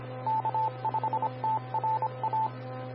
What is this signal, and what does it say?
Morse code, heard as a keyed tone, about 25 wpm, 880 Hz. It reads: K5TRA
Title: Austin, TX Amateur Repeaters